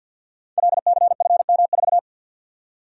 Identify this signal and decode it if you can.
Morse code, heard as a keyed tone, about 49 wpm, 700 Hz. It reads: LCFM4
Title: Code Groups - 5 Letter:Numbers - 50wpm